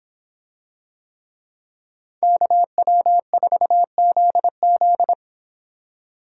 Morse code, heard as a keyed tone, about 26 wpm, 700 Hz. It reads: KW4ZZ